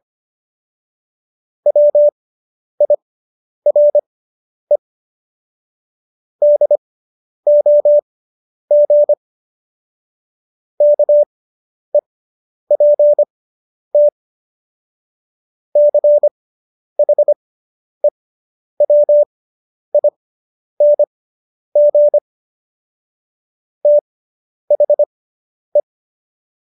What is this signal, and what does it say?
Morse code, heard as a keyed tone, about 25 wpm, spaced as for 10 wpm, 590 Hz. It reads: WIRE DOG KEPT CHEWING THE